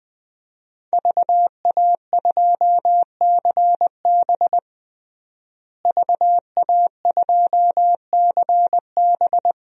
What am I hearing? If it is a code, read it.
Morse code, heard as a keyed tone, about 21 wpm, 700 Hz. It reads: VA2CB VA2CB
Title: DXCC-100 Call Signs - 5 Characters - Repeated 2x - 20wpm